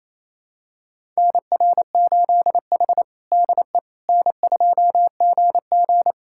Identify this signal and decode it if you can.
Morse code, heard as a keyed tone, about 28 wpm, 700 Hz. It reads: NR8H DE N2GG
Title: USA Callsign DE Callsign - 4 Characters - 28wpm